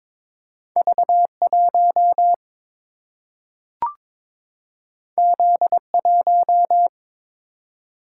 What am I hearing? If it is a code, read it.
Morse code, heard as a keyed tone, about 22 wpm, 700 Hz. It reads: V1 Z1
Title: Single-Letter Single-Number 22wpm